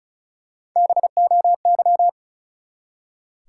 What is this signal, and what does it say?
Morse code, heard as a keyed tone, about 34 wpm, 700 Hz. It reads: BOY
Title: Three Letter Words - 35wpm